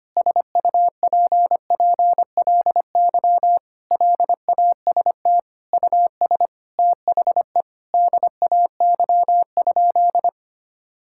Morse code, heard as a keyed tone, about 25 wpm, 700 Hz. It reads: SUPPLY LAST US THE DAY?
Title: Sentences from Top 1000 Words - Review Part 5 - 25wpm